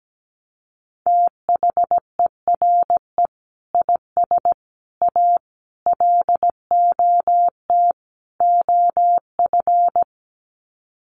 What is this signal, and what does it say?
Morse code, heard as a keyed tone, about 17 wpm, 700 Hz. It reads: THERE IS A LOT OF
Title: Sets of 5 Words 17wpm